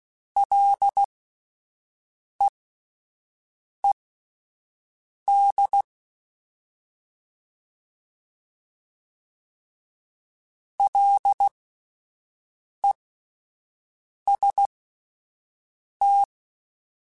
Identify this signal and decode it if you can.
Morse code, heard as a keyed tone, about 16 wpm, spaced as for 5 wpm, 800 Hz. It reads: LEED LEST
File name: Alphabetical / Alphabetical mp3s / Ls @5